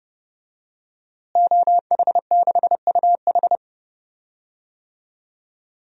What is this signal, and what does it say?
Morse code, heard as a keyed tone, about 31 wpm, 700 Hz. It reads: OH6UH
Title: DXCC-100 Call Signs - 5 Characters - Repeated 2x - 30wpm